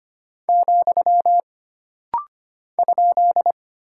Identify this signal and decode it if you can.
Morse code, heard as a keyed tone, about 25 wpm, 700 Hz. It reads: , ?
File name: Single Punctuation-Prosign - Rapid-Fire - 25wpm